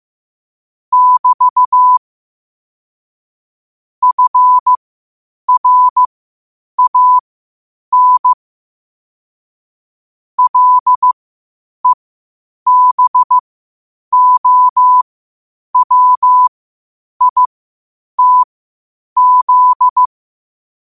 Morse code, heard as a keyed tone, about 15 wpm, spaced as for 8 wpm, 1000 Hz. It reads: <BT> FRAN LEBOWITZ
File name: qotdfiles / 8wpm